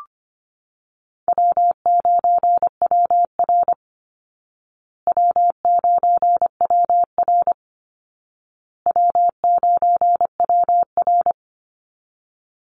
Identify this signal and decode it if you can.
Morse code, heard as a keyed tone, about 25 wpm, 700 Hz. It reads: W9WR W9WR W9WR
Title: Call Signs Repeated 2x - 25wpm